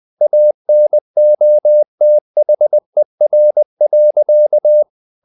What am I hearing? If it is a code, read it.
Morse code, heard as a keyed tone, about 20 wpm, 600 Hz. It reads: ANOTHER.
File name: the old man and the sea / the old man and the sea part 1 20wpm